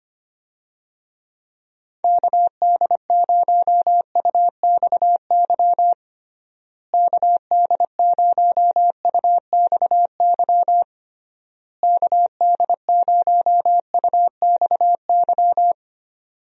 Morse code, heard as a keyed tone, about 25 wpm, 700 Hz. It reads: KD0UXY KD0UXY KD0UXY
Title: USA Call Signs - 6 Characters - Repeated 3x - 25wpm